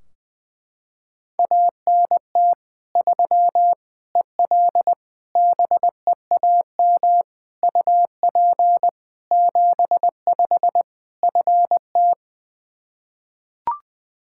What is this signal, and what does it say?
Morse code, heard as a keyed tone, about 20 wpm, 700 Hz. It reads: ANT 3 EL BEAM UP 75 FT
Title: Practice QSO - Exchange 2 - Info - 20wpm